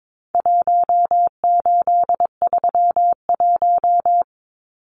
Morse code, heard as a keyed tone, about 22 wpm, 700 Hz. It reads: 1831
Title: Numbers - Contest Style - 4-Digit 22wpm